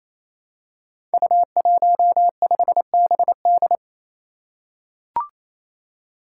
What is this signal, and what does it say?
Morse code, heard as a keyed tone, about 27 wpm, 700 Hz. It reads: U15BD
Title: Five Challenging Letter-Number Combinations 28wpm